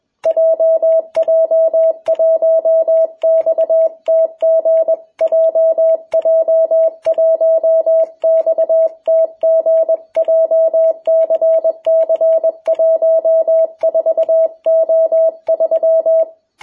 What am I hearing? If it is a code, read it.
Morse code, heard as a keyed tone, about 21 wpm, 660 Hz. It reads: JJ1XTG JJ1XTG JCC14O3
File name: jh8cbh / 2021cw